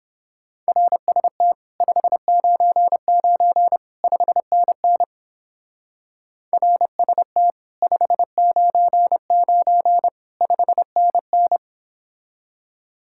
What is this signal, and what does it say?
Morse code, heard as a keyed tone, about 29 wpm, 700 Hz. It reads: RST 599 5NN RST 599 5NN